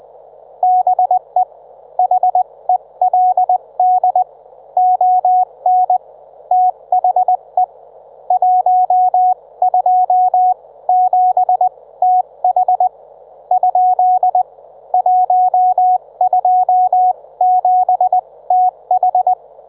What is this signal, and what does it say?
Morse code, heard as a keyed tone, about 20 wpm, spaced as for 17 wpm, 730 Hz. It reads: BE HELD ON THE 127TH ? 127TH
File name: Trivia